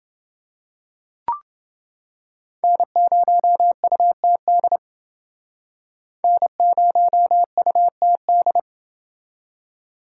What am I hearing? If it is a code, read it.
Morse code, heard as a keyed tone, about 29 wpm, 700 Hz. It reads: N0UTD N0UTD